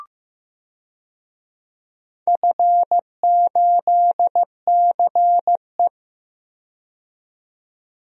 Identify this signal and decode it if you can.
Morse code, heard as a keyed tone, about 15 wpm, 700 Hz. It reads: F8CE
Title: DXCC-100 Call Signs - 4 Characters - 15wpm